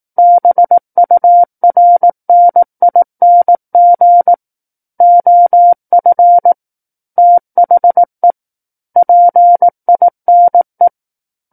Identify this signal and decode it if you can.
Morse code, heard as a keyed tone, about 18 wpm, 700 Hz. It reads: BURNING OF THE PINE
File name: 18 WPM